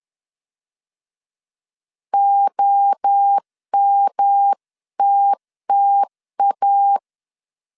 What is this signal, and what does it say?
Morse code, heard as a keyed tone, about 10 wpm, 800 Hz. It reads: OMTTA
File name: CW / DET / 1 3 1500C S C 10wpm 3335